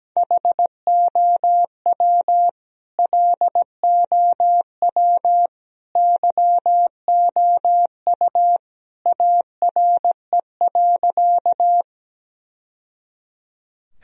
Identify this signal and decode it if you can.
Morse code, heard as a keyed tone, about 17 wpm, 700 Hz. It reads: HOW LOW YOU ARE.
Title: Sentences from Top 1000 Words - Review Part 1 - 17wpm